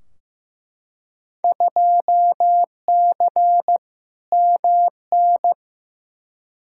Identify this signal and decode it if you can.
Morse code, heard as a keyed tone, about 15 wpm, 700 Hz. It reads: 2C MN
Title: Field Day Exchanges Encore - 15wpm